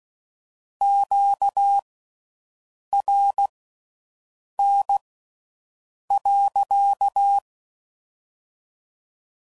Morse code, heard as a keyed tone, about 16 wpm, spaced as for 6 wpm, 800 Hz. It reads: QRN.